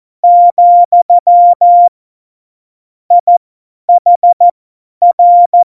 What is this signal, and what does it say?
Morse code, heard as a keyed tone, about 14 wpm, spaced as for 10 wpm, 700 Hz. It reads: , IHR
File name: Morsekurs / Witze / Witz 04. Juni 2019, 50 F70